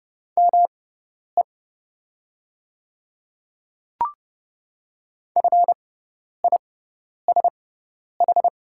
Morse code, heard as a keyed tone, about 29 wpm, spaced as for 10 wpm, 700 Hz. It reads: ME FISH
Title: 6x Character Spacing - Top 300 Words - 30wpm